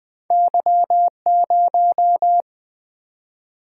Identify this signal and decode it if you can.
Morse code, heard as a keyed tone, about 20 wpm, 700 Hz. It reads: Y0
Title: Single-Letter Single-Number 20wpm